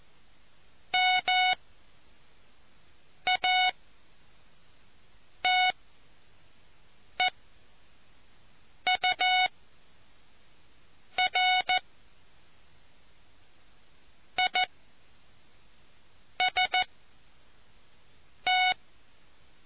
Morse code, heard as a keyed tone, about 14 wpm, spaced as for 4 wpm, 740 Hz. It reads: MATEURIST